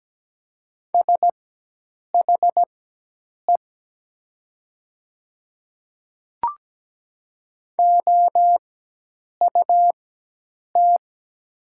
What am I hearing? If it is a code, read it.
Morse code, heard as a keyed tone, about 17 wpm, spaced as for 8 wpm, 700 Hz. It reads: SHE OUT
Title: 4x Character Spacing - Top 100 Words - 17wpm